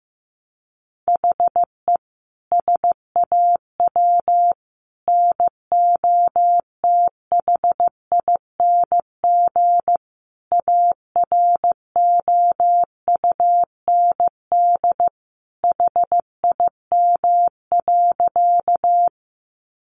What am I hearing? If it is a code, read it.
Morse code, heard as a keyed tone, about 15 wpm, 700 Hz. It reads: HE SAW NOTHING AROUND HIM.
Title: Sentences from Top 600 Words 15wpm